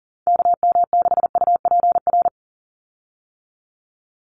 Morse code, heard as a keyed tone, about 40 wpm, 690 Hz. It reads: KM6UPR
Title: USA Call Signs 40wpm